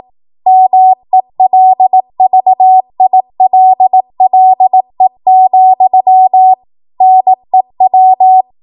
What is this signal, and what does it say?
Morse code, heard as a keyed tone, about 18 wpm, 750 Hz. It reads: MELVILLE, NEW